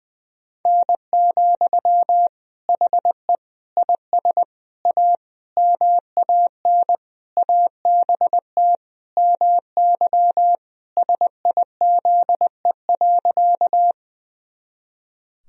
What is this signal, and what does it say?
Morse code, heard as a keyed tone, about 20 wpm, 700 Hz. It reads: N, HE IS A MAN ABT MY SIZE.
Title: Sentences from Top 400 Words and Abbreviations 20wpm